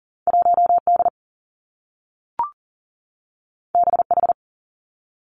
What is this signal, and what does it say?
Morse code, heard as a keyed tone, about 42 wpm, 700 Hz. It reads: 1 D B H
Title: Two Challenging Letter-Number Combinations 40wpm